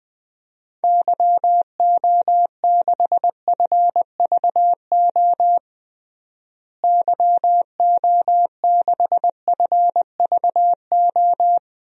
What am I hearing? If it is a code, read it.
Morse code, heard as a keyed tone, about 20 wpm, 700 Hz. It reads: YO6FVO YO6FVO
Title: DXCC-100 Call Signs - 6 Characters - Repeated 3x - 20wpm